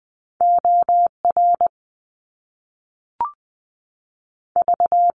Morse code, heard as a keyed tone, about 20 wpm, spaced as for 5 wpm, 700 Hz. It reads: ORV
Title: US States 20wpm